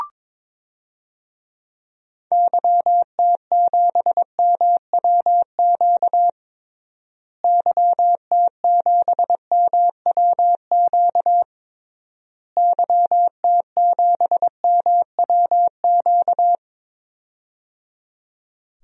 Morse code, heard as a keyed tone, about 22 wpm, 700 Hz. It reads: YT7MWQ YT7MWQ YT7MWQ
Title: DXCC-100 Call Signs - 6 Characters - Repeated 3x - 22wpm